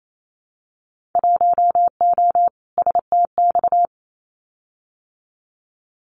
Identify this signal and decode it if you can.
Morse code, heard as a keyed tone, about 28 wpm, 700 Hz. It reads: 1O STX
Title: Winter Field Day Exchanges 28wpm